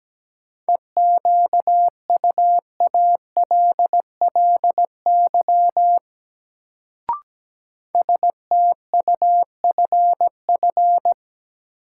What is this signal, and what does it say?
Morse code, heard as a keyed tone, about 17 wpm, 700 Hz. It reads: EQUALLY STUFF
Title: Sentences from Top 1400 Words - New Word Drill Alt - 17wpm